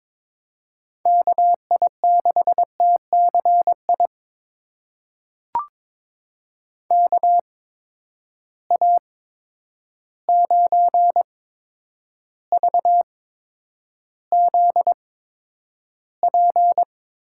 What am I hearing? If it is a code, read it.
Morse code, heard as a keyed tone, about 22 wpm, spaced as for 9 wpm, 700 Hz. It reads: KI6TCI KA9VZP
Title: USA Call Signs - 6 Characters - 8x Character Spacing - 22wpm